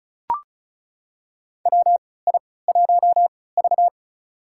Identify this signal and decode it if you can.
Morse code, heard as a keyed tone, about 34 wpm, spaced as for 20 wpm, 700 Hz. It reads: WI1V
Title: USA Call Signs - 4 Characters - 3x Character Spacing - 35wpm